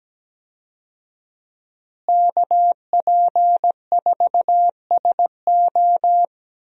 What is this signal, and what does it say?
Morse code, heard as a keyed tone, about 17 wpm, 700 Hz. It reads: KP4SO
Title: DXCC-100 Call Signs - 5 Characters - Repeated 2x - 17wpm